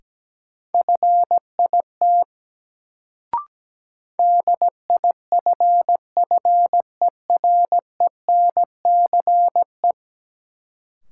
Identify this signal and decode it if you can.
Morse code, heard as a keyed tone, about 17 wpm, 700 Hz. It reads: FIT DIFFERENCE
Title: Sentences from Top 800 Words - New Word Drill - 17wpm